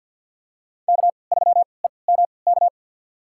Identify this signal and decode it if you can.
Morse code, heard as a keyed tone, about 45 wpm, spaced as for 27 wpm, 700 Hz. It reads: K3EMK
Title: USA Call Signs - 5 Characters - 3x Character Spacing - 50wpm